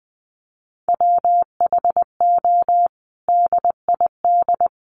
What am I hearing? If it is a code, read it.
Morse code, heard as a keyed tone, about 19 wpm, 700 Hz. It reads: WHO DID
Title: Sentences from Top 300 Words 20wpm